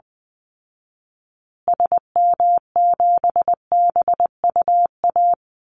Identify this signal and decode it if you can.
Morse code, heard as a keyed tone, about 20 wpm, 700 Hz. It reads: SM7BUA
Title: Call Signs Repeated 2x - 20wpm